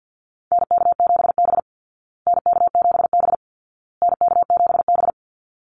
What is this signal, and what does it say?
Morse code, heard as a keyed tone, about 50 wpm, 700 Hz. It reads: NK7B NK7B NK7B
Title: Call Signs Repeated 2x - 50wpm